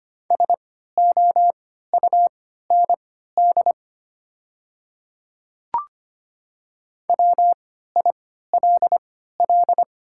Morse code, heard as a keyed tone, about 25 wpm, spaced as for 14 wpm, 700 Hz. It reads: SOUND WILL
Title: 3x Character Spacing - Top 100 Words - 25wpm